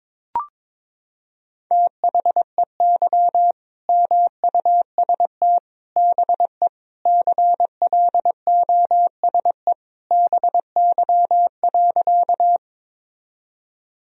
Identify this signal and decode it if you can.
Morse code, heard as a keyed tone, about 22 wpm, 700 Hz. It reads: THEY MUST BE CLOSE BY.
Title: Sentences from Top 1000 Words - Review Part 1 - 22wpm